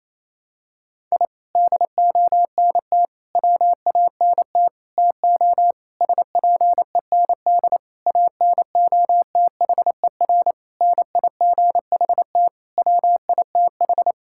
Morse code, heard as a keyed tone, about 28 wpm, 700 Hz. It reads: I DONT WANT TO SPEND ANOTHER NIGHT WITH
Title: Sentences from Top 1000 Words - Review Part 5 - 28wpm